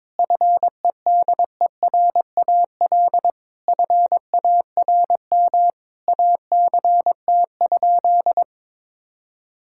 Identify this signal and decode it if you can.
Morse code, heard as a keyed tone, about 22 wpm, 700 Hz. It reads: FEDERAL FARM ACT?
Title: Sentences from Top 1000 Words - Review Part 5 - 22wpm